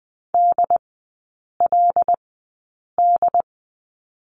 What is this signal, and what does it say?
Morse code, heard as a keyed tone, about 20 wpm, spaced as for 8 wpm, 700 Hz. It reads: DLD
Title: 178 Lesson 29e - Slash - Three Characters 20wpm